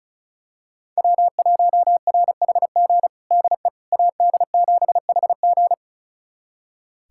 Morse code, heard as a keyed tone, about 34 wpm, 700 Hz. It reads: W1RHG DE AD7HG
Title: USA Callsign DE Callsign - 5 Characters - 35wpm